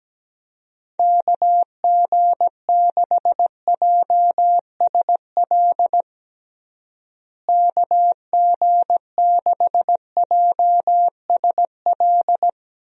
Morse code, heard as a keyed tone, about 17 wpm, 700 Hz. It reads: KG6JSL KG6JSL
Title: USA Call Signs - 6 Characters - Repeated 3x - 17wpm